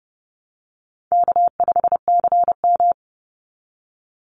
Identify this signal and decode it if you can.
Morse code, heard as a keyed tone, about 30 wpm, 700 Hz. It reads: K5CM
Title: Easy Call Signs Repeated 3x - 30wpm